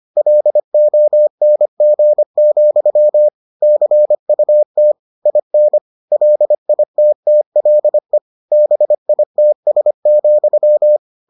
Morse code, heard as a keyed tone, about 25 wpm, 600 Hz. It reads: LONG, CUT IN LITTLE BITS,